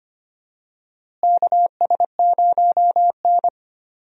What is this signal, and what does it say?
Morse code, heard as a keyed tone, about 25 wpm, 700 Hz. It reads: KS0N
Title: USA Call Signs - 4 Characters - Repeated 2x - 25wpm